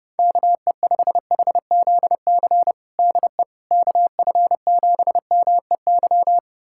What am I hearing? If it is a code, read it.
Morse code, heard as a keyed tone, about 30 wpm, 700 Hz. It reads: KE5HZC DE KF7MEY
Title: USA Callsign DE Callsign - 6 Characters - 30wpm